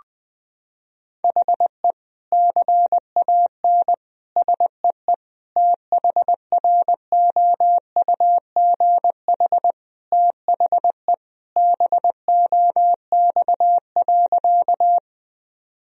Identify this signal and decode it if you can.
Morse code, heard as a keyed tone, about 19 wpm, 700 Hz. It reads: HE CAN SEE THROUGH THE BOX.
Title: Sentences from Top 1000 Words - Review Part 2 - 20wpm